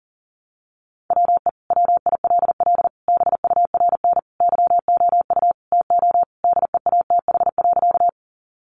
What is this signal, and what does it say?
Morse code, heard as a keyed tone, about 40 wpm, 700 Hz. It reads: WE WILL BURN YOU TO DEATH.
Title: Sentences from Top 1300 Words 40wpm